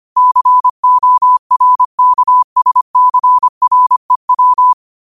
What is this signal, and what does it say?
Morse code, heard as a keyed tone, about 25 wpm, 1000 Hz. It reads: CORKSCREW
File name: qotdfiles / 25wpm